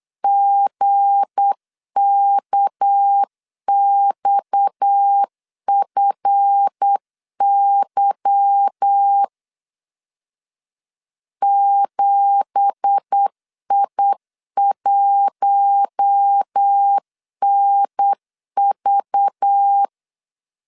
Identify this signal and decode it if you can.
Morse code, heard as a keyed tone, about 8 wpm, 800 Hz. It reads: GKXFY 7I1NV
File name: 8wpm